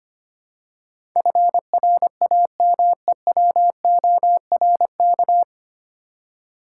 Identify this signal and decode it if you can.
Morse code, heard as a keyed tone, about 25 wpm, 700 Hz. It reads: FRAMEWORK
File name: Nine Letter Words - 25wpm